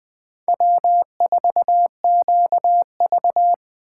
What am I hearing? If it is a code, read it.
Morse code, heard as a keyed tone, about 20 wpm, 700 Hz. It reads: W4QV
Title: Speed Racing - USA 1x2 Call Signs 17wpm